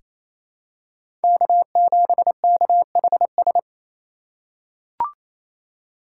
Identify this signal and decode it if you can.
Morse code, heard as a keyed tone, about 28 wpm, 700 Hz. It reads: K7KHS
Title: USA Call Signs - 5 Characters - 6x Character Spacing - 28wpm